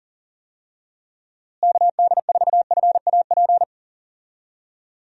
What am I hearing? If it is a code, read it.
Morse code, heard as a keyed tone, about 38 wpm, 700 Hz. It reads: KD4FAP
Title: USA Call Signs - 6 Characters - Repeated 2x - 40wpm